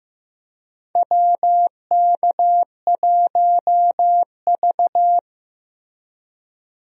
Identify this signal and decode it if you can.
Morse code, heard as a keyed tone, about 16 wpm, 700 Hz. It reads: WK1V
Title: USA Call Signs - 4 Characters - Repeated 2x - 15wpm